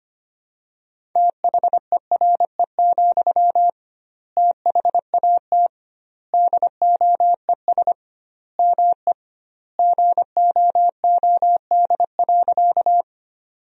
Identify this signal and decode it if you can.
Morse code, heard as a keyed tone, about 25 wpm, 700 Hz. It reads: THERE, THAT DOES ME GOOD.
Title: 2x Word-Spacing - Sentences from Top 200 Words - 25wpm